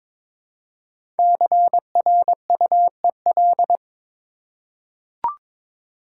Morse code, heard as a keyed tone, about 22 wpm, 700 Hz. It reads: CRUEL
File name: 5 Letter Words - First 3 Letters Given - 22wpm